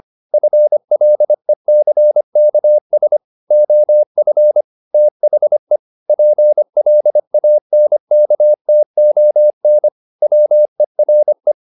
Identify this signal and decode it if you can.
Morse code, heard as a keyed tone, about 25 wpm, 600 Hz. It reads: FLECKS OF THE PLANKTON WERE
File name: the old man and the sea part 1 25wpm